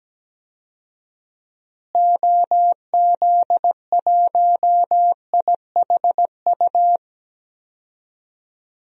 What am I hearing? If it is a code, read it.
Morse code, heard as a keyed tone, about 17 wpm, 700 Hz. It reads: OZ1IHU
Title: DXCC-100 Call Signs - 6 Characters - Repeated 2x - 17wpm